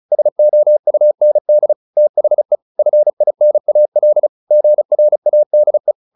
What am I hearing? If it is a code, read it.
Morse code, heard as a keyed tone, about 34 wpm, 600 Hz. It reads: SOUND THE FINAL GRADE
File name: PR401 35